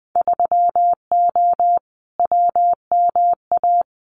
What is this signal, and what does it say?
Morse code, heard as a keyed tone, about 20 wpm, 700 Hz. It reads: 3O WMA
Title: Winter Field Day Exchanges Encore 20wpm